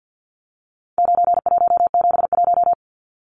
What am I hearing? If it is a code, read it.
Morse code, heard as a keyed tone, about 50 wpm, 700 Hz. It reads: 9171
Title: Numbers - 4-Digit 50wpm